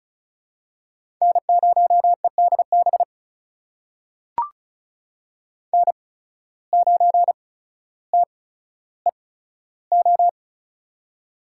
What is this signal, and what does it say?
Morse code, heard as a keyed tone, about 36 wpm, spaced as for 15 wpm, 700 Hz. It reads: N0EDB N9TEO
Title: USA Call Signs - 5 Characters - 8x Character Spacing - 35wpm